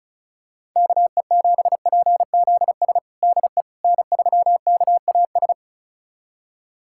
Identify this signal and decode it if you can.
Morse code, heard as a keyed tone, about 34 wpm, 700 Hz. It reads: KE7PZS DE N3KAS
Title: USA Callsign DE Callsign 35wpm